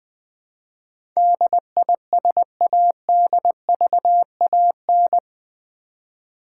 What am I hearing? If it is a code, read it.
Morse code, heard as a keyed tone, about 20 wpm, 700 Hz. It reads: DISADVAN
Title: Build A Word - 4 Letters At A Time - 12-Letter Words - 20wpm